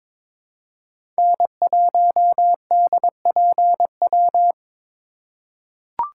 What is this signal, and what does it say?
Morse code, heard as a keyed tone, about 21 wpm, 700 Hz. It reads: N1DPW
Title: USA Call Signs - 4x Character Spacing - 22wpm